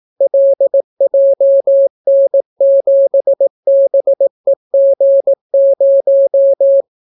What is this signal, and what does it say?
Morse code, heard as a keyed tone, about 18 wpm, 550 Hz. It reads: LJN7BEG0